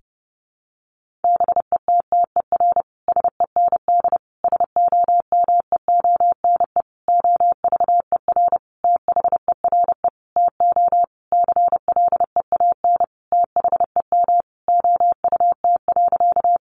Morse code, heard as a keyed tone, about 30 wpm, 700 Hz. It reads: BETTER SEND SOMEONE OVER THERE TO CLEAN THEM OUT.
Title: Sentences from Top 1500 Words 30wpm